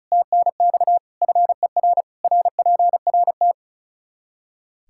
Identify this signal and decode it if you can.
Morse code, heard as a keyed tone, about 36 wpm, 700 Hz. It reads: TNX FER RPRT
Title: Practice QSO - Exchange 2 - Info - 35wpm